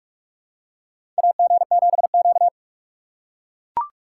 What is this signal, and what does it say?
Morse code, heard as a keyed tone, about 44 wpm, 700 Hz. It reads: AG7Q
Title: USA Call Signs - 4 Characters - 3x Character Spacing - 45wpm